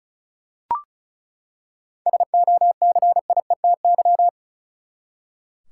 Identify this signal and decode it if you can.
Morse code, heard as a keyed tone, about 36 wpm, 700 Hz. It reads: SOCIETY
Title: Seven Letter Words - 35wpm